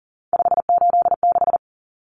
Morse code, heard as a keyed tone, about 40 wpm, 700 Hz. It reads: 586
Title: Numbers - 3-Digit 40wpm